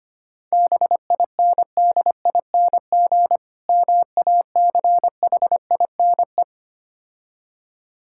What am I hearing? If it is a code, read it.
Morse code, heard as a keyed tone, about 25 wpm, 700 Hz. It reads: BINDING MACHINE
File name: practice / 25wpm / Difficult Words - Speed-Racing with Hints - 25wpm